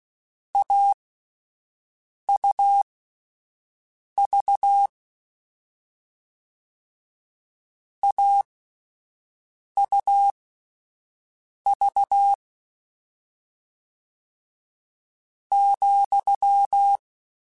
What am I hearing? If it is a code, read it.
Morse code, heard as a keyed tone, about 16 wpm, spaced as for 5 wpm, 800 Hz. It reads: AUV AUV ,
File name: Difficult Combos / Difficult Combos-mp3s / AUV AUV4 @5